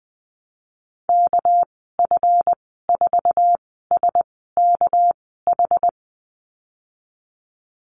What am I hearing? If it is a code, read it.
Morse code, heard as a keyed tone, about 20 wpm, spaced as for 15 wpm, 700 Hz. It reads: KF4SKH
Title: 219 Lesson 35g - Number 0 - Call Signs 20wpm